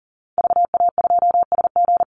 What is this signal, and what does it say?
Morse code, heard as a keyed tone, about 40 wpm, 700 Hz. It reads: VA2SG
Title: Call Signs Repeated 2x - 40wpm